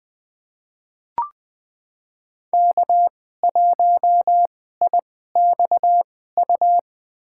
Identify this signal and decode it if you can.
Morse code, heard as a keyed tone, about 20 wpm, spaced as for 14 wpm, 700 Hz. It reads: K1IXU
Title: USA Call Signs - 5 Characters - 2x Character Spacing - 20wpm